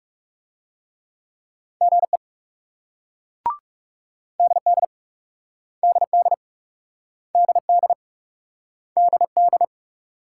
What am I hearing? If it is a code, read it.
Morse code, heard as a keyed tone, about 42 wpm, 700 Hz. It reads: GE DD DD DD DD